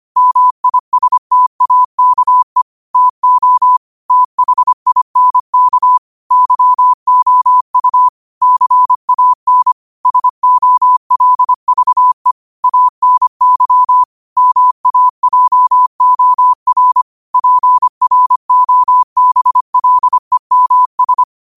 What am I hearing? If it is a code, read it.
Morse code, heard as a keyed tone, about 25 wpm, 1000 Hz. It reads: MISTAKE TO THINK YOU CAN SOLVE ANY MAJOR PROBLEMS